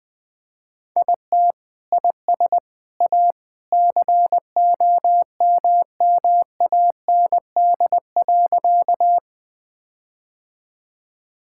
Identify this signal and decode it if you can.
Morse code, heard as a keyed tone, about 20 wpm, 700 Hz. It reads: IT IS A COMMAND.
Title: Sentences from Top 1600 Words 20wpm